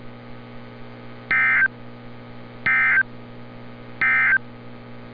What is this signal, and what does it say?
Emergency Alert System (SAME) tones, heard as harsh buzzing bursts. It is heard from 1.2 to 4.5 s.
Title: NOAA Weather Radio - Altoona, PA